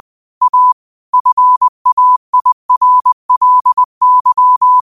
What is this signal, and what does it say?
Morse code, heard as a keyed tone, about 20 wpm, 1000 Hz. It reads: A FAIRLY
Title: 20 WPM morse code quotes for Thu, 29 Feb 2024